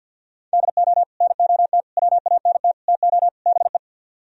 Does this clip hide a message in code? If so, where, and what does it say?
Yes: Morse code from 0.5 to 3.8 s, heard as a keyed tone, about 64 wpm, 700 Hz. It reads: DO NOT WANT TO BE